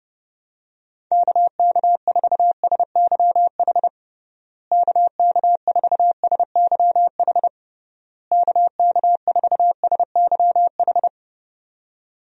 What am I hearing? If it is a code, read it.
Morse code, heard as a keyed tone, about 30 wpm, 700 Hz. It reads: KK4SYH KK4SYH KK4SYH
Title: USA Call Signs - 6 Characters - Repeated 3x - 30wpm